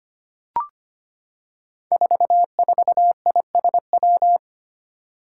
Morse code, heard as a keyed tone, about 25 wpm, 700 Hz. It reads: 44ISW
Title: Five Challenging Letter-Number Combinations 25wpm